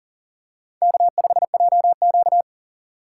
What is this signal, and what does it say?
Morse code, heard as a keyed tone, about 40 wpm, 700 Hz. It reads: K5JQ
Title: USA Call Signs - 4 Characters - 6x Character Spacing - 40wpm